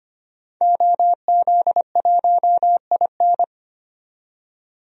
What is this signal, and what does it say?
Morse code, heard as a keyed tone, about 25 wpm, 700 Hz. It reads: OZ1IN
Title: DXCC-100 Call Signs - 5 Characters - 25wpm